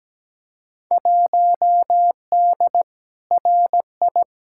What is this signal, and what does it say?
Morse code, heard as a keyed tone, about 17 wpm, 700 Hz. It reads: 1D RI
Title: Field Day Exchanges - 17wpm